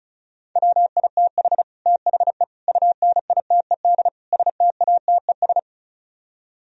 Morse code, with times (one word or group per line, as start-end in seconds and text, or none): 0.6-1.6 WITH
1.9-2.4 THE
2.7-4.1 UNITED
4.3-5.6 STATES